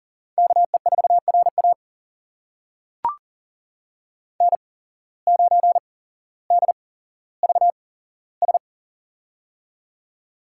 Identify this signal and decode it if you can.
Morse code, heard as a keyed tone, about 40 wpm, spaced as for 17 wpm, 700 Hz. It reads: KE4RA N9DVS
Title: USA Call Signs - 5 Characters - 8x Character Spacing - 40wpm